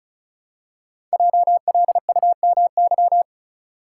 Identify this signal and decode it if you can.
Morse code, heard as a keyed tone, about 36 wpm, 700 Hz. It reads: JLUMY
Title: Code Groups - 5 Letters - 35wpm